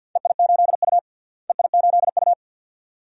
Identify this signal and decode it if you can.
Morse code, heard as a keyed tone, about 47 wpm, 700 Hz. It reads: EI8U EI8U
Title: DXCC-100 Call Signs - 4 Characters - Repeated 2x - 50wpm